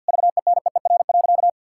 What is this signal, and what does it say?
Morse code, heard as a keyed tone, about 51 wpm, 700 Hz. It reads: VENEER.
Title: Pangram Challenge v2 - 50wpm